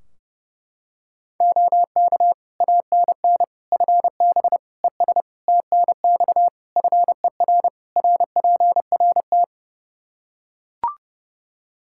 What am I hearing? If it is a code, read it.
Morse code, heard as a keyed tone, about 30 wpm, 700 Hz. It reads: OK ANN FB ES TNX FER RPRT
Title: K1USN SST QSO Practice 30wpm